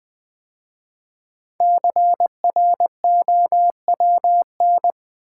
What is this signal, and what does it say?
Morse code, heard as a keyed tone, about 20 wpm, 700 Hz. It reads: CROWN
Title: 5 Letter Words - First 3 Letters Given - 20wpm